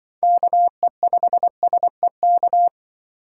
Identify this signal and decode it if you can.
Morse code, heard as a keyed tone, about 25 wpm, 700 Hz. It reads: KE5SEK
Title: Speed Racing - USA 2x3 Call Signs 20wpm